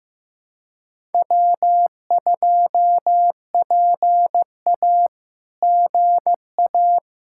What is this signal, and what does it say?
Morse code, heard as a keyed tone, about 15 wpm, 700 Hz. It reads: W2PA GA
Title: SOTA QSO Practice 15wpm